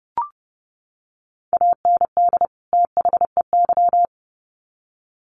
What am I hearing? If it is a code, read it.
Morse code, heard as a keyed tone, about 30 wpm, 700 Hz. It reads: AND THEY
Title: Sets of 2 Words 30wpm